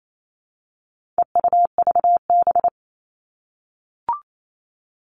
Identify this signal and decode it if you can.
Morse code, heard as a keyed tone, about 28 wpm, 700 Hz. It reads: EUVB
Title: All Cut Numbers - 4-Digit 28wpm